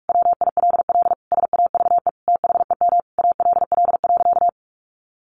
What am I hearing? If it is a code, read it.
Morse code, heard as a keyed tone, about 44 wpm, 700 Hz. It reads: WILL SAVE THEM ALL.
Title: Sentences from Top 1100 Words 45wpm